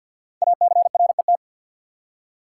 Morse code, heard as a keyed tone, about 50 wpm, 690 Hz. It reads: ACRET
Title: Code Groups - 5 Letters - 50wpm